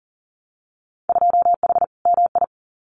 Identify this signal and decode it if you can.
Morse code, heard as a keyed tone, about 40 wpm, 700 Hz. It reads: 2H MI
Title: Winter Field Day Exchanges 40wpm